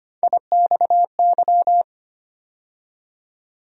Morse code, heard as a keyed tone, about 25 wpm, 700 Hz. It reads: IXY
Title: USA Call Sign Postfixes - 3 Letters 25wpm